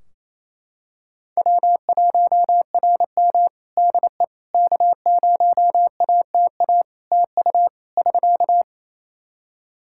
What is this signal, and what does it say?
Morse code, heard as a keyed tone, about 27 wpm, 700 Hz. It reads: W1RM DE K0ATA TU <SK>
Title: Practice QSO - Exchange 4 - Ending - 28wpm